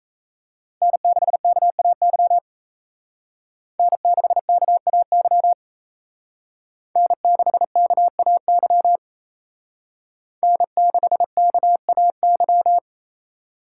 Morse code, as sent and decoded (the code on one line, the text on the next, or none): -. -.... -.- .- -.-- / -. -.... -.- .- -.-- / -. -.... -.- .- -.-- / -. -.... -.- .- -.--
N6KAY N6KAY N6KAY N6KAY